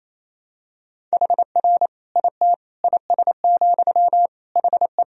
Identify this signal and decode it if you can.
Morse code, heard as a keyed tone, about 28 wpm, 700 Hz. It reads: HR IT IS, HE
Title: Sentences from Top 200 Words and Abbreviations 28wpm